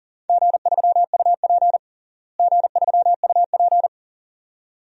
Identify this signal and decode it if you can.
Morse code, heard as a keyed tone, about 40 wpm, 700 Hz. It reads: G3UP G3UP
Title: DXCC-100 Call Signs - 4 Characters - Repeated 2x - 40wpm